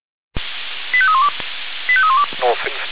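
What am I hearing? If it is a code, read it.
Five-tone selective call (ZVEI), heard as quick runs of five tones, tones of about 70 ms. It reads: 85312 85312
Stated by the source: ZVEI five-tone selective call received on an SDR.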